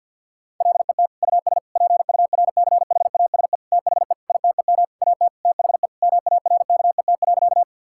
Morse code, heard as a keyed tone, about 50 wpm, 700 Hz. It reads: LET US PURCHASE THE ITEM AT THE MARKET.
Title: Sentences from Top 1000 Words - Review Part 6 - 50wpm